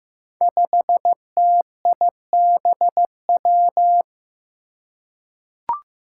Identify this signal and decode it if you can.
Morse code, heard as a keyed tone, about 15 wpm, 700 Hz. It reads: 5TIBW
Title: Five Challenging Letter-Number Combinations 15wpm